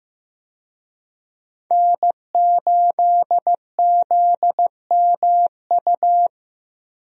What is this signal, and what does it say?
Morse code, heard as a keyed tone, about 15 wpm, 700 Hz. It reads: N8ZMU
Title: USA Call Signs - 5 Characters - Repeated 3x - 15wpm